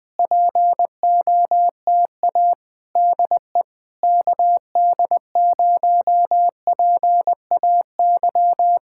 Morse code, heard as a keyed tone, about 20 wpm, 700 Hz. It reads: POTA DE KD0PAY
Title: Speed-Racing Common QSO Phrases - 17wpm